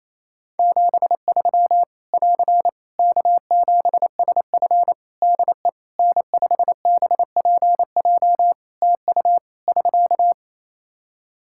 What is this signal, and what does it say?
Morse code, heard as a keyed tone, about 27 wpm, 700 Hz. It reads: 73 <AR> K7SF DE N5BPJ TU <SK>
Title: Practice QSO - Exchange 4 - Ending - 28wpm